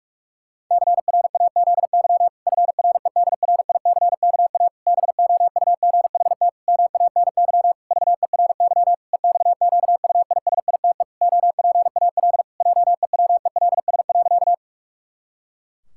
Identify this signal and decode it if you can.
Morse code, heard as a keyed tone, about 46 wpm, 700 Hz. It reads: CRAZY FREDRICKA BOUGHT MANY VERY EXQUISITE OPAL JEWELS.
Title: Pangram Challenge v2 - 45wpm